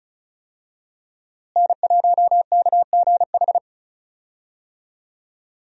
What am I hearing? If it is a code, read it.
Morse code, heard as a keyed tone, about 34 wpm, 700 Hz. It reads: N1KGH
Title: USA Call Signs - 5 Characters - Repeated 2x - 35wpm